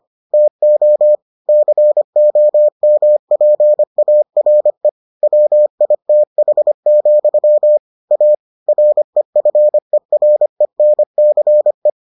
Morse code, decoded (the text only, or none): TO COMPARE WITH, A REFERENCE